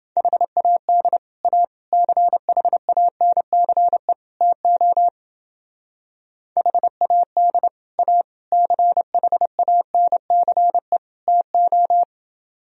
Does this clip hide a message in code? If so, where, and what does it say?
Yes: Morse code from 0.2 to 12.0 s, heard as a keyed tone, about 29 wpm, 700 Hz. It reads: HAD A CHANCE TO HAD A CHANCE TO